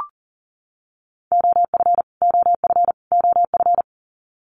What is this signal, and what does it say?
Morse code, heard as a keyed tone, about 40 wpm, 700 Hz. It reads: OF OF OF
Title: Top 500 Words Repeated 40wpm